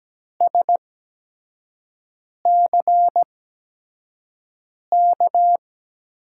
Morse code, heard as a keyed tone, about 17 wpm, spaced as for 5 wpm, 700 Hz. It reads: SCK
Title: USA Call Signs - 6 Characters - 8x Character Spacing - 17wpm